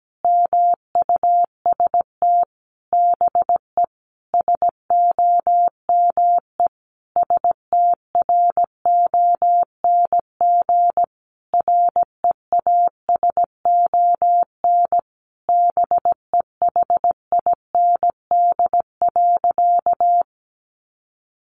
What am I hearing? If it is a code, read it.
Morse code, heard as a keyed tone, about 17 wpm, 700 Hz. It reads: MUST BE SOME STRONG REASON BEHIND.
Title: Sentences from Top 700 Words 17wpm